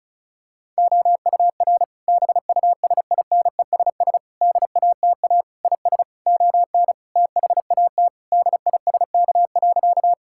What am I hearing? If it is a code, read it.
Morse code, heard as a keyed tone, about 36 wpm, 690 Hz. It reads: OUR BUSINESS DATA IS ON THAT DISK.
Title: Sentences from Top 1000 Words - Review Part 6 - 35wpm